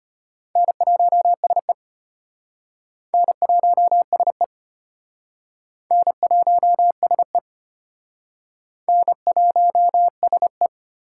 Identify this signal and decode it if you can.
Morse code, heard as a keyed tone, about 33 wpm, 700 Hz. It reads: N1SE N1SE N1SE N1SE